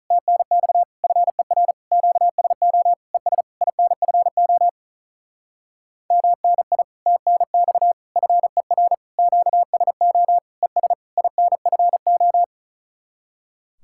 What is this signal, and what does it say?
Morse code, heard as a keyed tone, about 44 wpm, 700 Hz. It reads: TNX FER QSO ES INFO MNI TNX FER QSO ES INFO